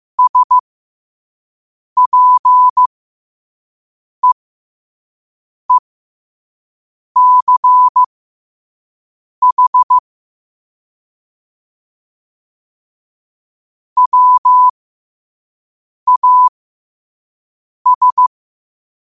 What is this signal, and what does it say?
Morse code, heard as a keyed tone, about 15 wpm, spaced as for 5 wpm, 1000 Hz. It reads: SPEECH WAS
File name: qotdfiles / 5wpm